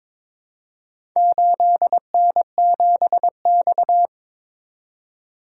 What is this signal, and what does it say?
Morse code, heard as a keyed tone, about 22 wpm, 700 Hz. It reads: 8N7X
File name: practice / 22wpm / Four Letter-Number Combinations - 22wpm